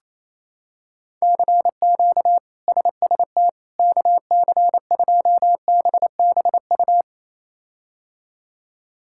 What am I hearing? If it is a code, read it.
Morse code, heard as a keyed tone, about 28 wpm, 700 Hz. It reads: CQ SST KC2BBU
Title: K1USN SST QSO Practice - 28wpm